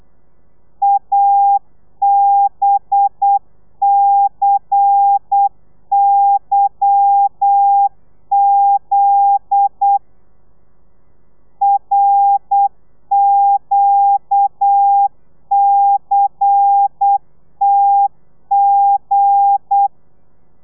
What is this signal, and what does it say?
Morse code, heard as a keyed tone, about 8 wpm, 800 Hz. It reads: ABCYZ RQCTG
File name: telegrafia / i2tfj / Lez10 35